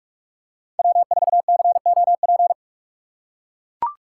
Morse code, heard as a keyed tone, about 41 wpm, 700 Hz. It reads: W4COP
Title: USA Call Signs - 8x Character Spacing - 45wpm